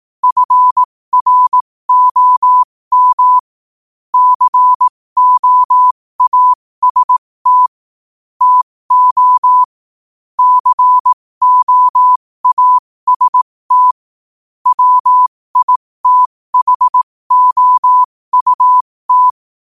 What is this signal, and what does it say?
Morse code, heard as a keyed tone, about 18 wpm, spaced as for 15 wpm, 1000 Hz. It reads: FROM COAST TO COAST WITHOUT
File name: qotdfiles / 15wpm